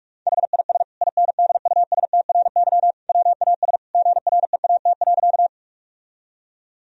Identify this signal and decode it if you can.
Morse code, heard as a keyed tone, about 46 wpm, 700 Hz. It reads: HIS INDUSTRY WAS GREAT.
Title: Sentences from Top 1000 Words - Review Part 4 - 45wpm